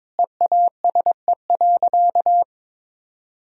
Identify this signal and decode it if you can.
Morse code, heard as a keyed tone, about 22 wpm, 700 Hz. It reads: EASE.
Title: Sentences from Top 1000 Words - Review Part 1 - 22wpm